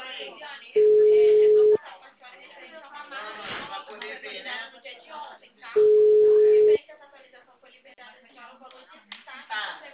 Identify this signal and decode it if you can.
Ringback tone (Europe, 425 Hz), with speech.
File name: out- -2009- - - .6